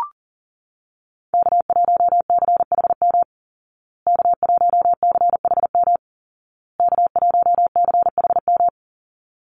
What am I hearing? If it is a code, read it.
Morse code, heard as a keyed tone, about 40 wpm, 700 Hz. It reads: K1CHM K1CHM K1CHM
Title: Call Signs Repeated 2x - 40wpm